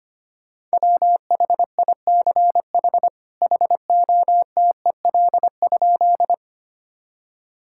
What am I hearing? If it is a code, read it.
Morse code, heard as a keyed tone, about 25 wpm, 700 Hz. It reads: WHICH HOTEL?
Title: Sentences from Top 1600 Words 25wpm